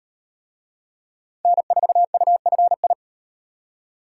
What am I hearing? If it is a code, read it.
Morse code, heard as a keyed tone, about 35 wpm, 700 Hz. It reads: N4UFI